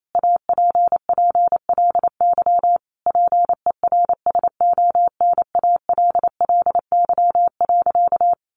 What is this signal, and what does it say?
Morse code, heard as a keyed tone, about 28 wpm, 700 Hz. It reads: APPLY PERSONALLY.
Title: Sentences from Top 1400 Words 28wpm